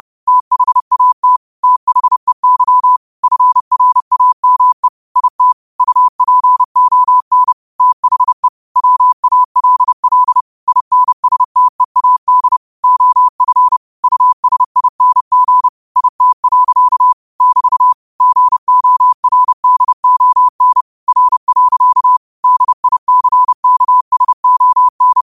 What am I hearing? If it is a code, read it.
Morse code, heard as a keyed tone, about 30 wpm, 1000 Hz. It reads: THAT THEY FRAME IT UPON THE WALL INSTEAD OF USING IT. <BT> GORDON R. DICKSON
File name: qotdfiles / 30wpm